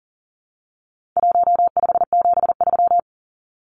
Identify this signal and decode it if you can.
Morse code, heard as a keyed tone, about 40 wpm, 700 Hz. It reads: 1573
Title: Numbers - Contest Style - 4-Digit 40wpm